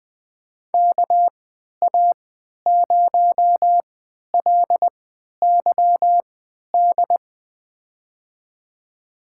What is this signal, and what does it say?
Morse code, heard as a keyed tone, about 20 wpm, spaced as for 11 wpm, 700 Hz. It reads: KA0LYD
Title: USA Call Signs - 6 Characters - 3x Character Spacing - 20wpm